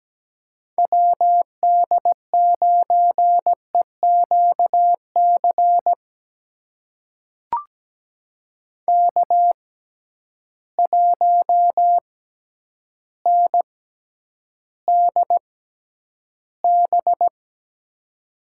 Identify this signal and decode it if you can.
Morse code, heard as a keyed tone, about 17 wpm, spaced as for 9 wpm, 700 Hz. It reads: WD9EQC K1NDB
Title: USA Call Signs - 6x Character Spacing - 17wpm